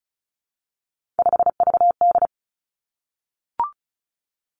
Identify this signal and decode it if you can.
Morse code, heard as a keyed tone, about 36 wpm, 700 Hz. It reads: 5VD
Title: Three Challenging Letter-Number Combinations 35wpm